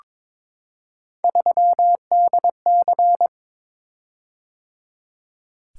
Morse code, heard as a keyed tone, about 22 wpm, 700 Hz. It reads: 3DC
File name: practice / 22wpm / Three Letter-Number Combinations - 22wpm